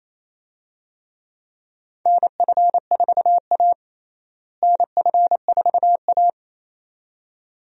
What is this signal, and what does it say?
Morse code, heard as a keyed tone, about 28 wpm, 700 Hz. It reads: NF4A NF4A
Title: USA Call Signs - 4 Characters - Repeated 2x - 28wpm